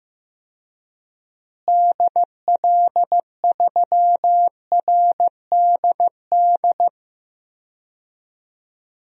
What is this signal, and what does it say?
Morse code, heard as a keyed tone, about 15 wpm, 700 Hz. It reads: DL3RDD
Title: DXCC-100 Call Signs - 6 Characters - Repeated 3x - 15wpm